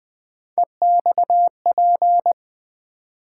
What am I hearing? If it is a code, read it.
Morse code, heard as a keyed tone, about 20 wpm, 700 Hz. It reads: EXP
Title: Build A Word - 3 Letters At A Time - 6-Letter Words - 20wpm